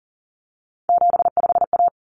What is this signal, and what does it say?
Morse code, heard as a keyed tone, about 40 wpm, 700 Hz. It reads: 75A
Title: Common Cut Numbers - 3-Digit 40wpm